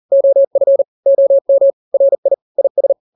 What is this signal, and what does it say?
Morse code, heard as a keyed tone, about 40 wpm, 550 Hz. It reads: OF OM RI IS
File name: n-grams / 2-grams 40 35 WPM